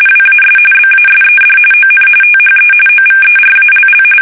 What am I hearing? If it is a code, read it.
RTTY, about 45.45 baud. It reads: WAS BASED ON THE OVERALL
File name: fsk850 / usnavy5